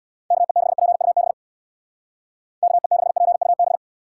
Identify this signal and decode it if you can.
Morse code, heard as a keyed tone, about 64 wpm, 700 Hz. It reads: KE6PUB KE6PUB